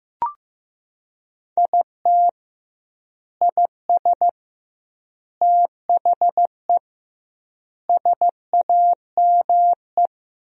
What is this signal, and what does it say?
Morse code, heard as a keyed tone, about 14 wpm, 700 Hz. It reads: IT IS THE SAME
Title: 2x Word-Spacing - Sentences from Top 200 Words - 15wpm